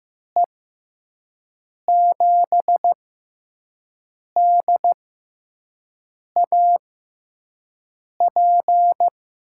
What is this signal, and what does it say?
Morse code, heard as a keyed tone, about 15 wpm, spaced as for 5 wpm, 700 Hz. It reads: E7DAP
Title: USA Call Signs - 6 Characters - 6x Character Spacing - 15wpm